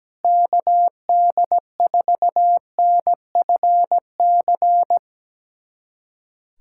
Morse code, heard as a keyed tone, about 17 wpm, 700 Hz. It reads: KD4NFC
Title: USA Callsign DE Callsign - 6 Characters - 17wpm